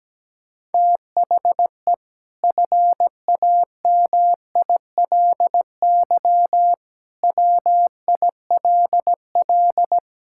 Morse code, heard as a keyed tone, about 17 wpm, 700 Hz. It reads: THE FAMILY WILL
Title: Sentences from Top 1000 Words - Review Part 6 - 17wpm